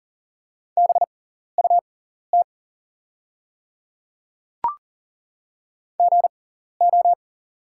Morse code, heard as a keyed tone, about 40 wpm, spaced as for 14 wpm, 700 Hz. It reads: BUT GO
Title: 6x Character Spacing - Top 100 Words - 40wpm